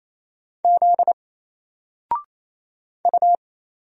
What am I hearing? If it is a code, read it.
Morse code, heard as a keyed tone, about 27 wpm, 700 Hz. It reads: Z U
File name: Single-Letter-Number-28wpm